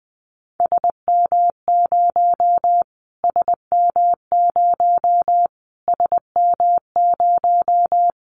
Morse code, heard as a keyed tone, about 20 wpm, 700 Hz. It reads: SM0 SM0 SM0
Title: Callsign Prefixes for Top 25 DXCC - Repeated 3x - 20wpm